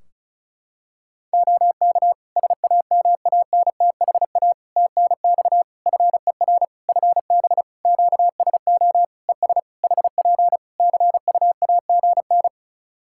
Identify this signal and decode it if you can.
Morse code, heard as a keyed tone, about 36 wpm, 700 Hz. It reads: OK SAMANTHA TNX FER FB QSO ES HP CUAGN
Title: Practice QSO - Exchange 4 - Ending - 35wpm